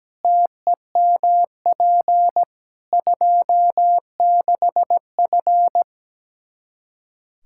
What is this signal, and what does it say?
Morse code, heard as a keyed tone, about 17 wpm, 700 Hz. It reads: TEMP 26F
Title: Practice QSO - Exchange 2 - Info - 17wpm